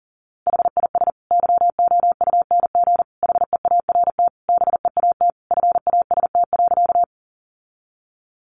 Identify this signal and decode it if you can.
Morse code, heard as a keyed tone, about 40 wpm, 700 Hz. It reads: HIS YOUNG HEART BEAT FAST.
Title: Sentences from Top 1300 Words 40wpm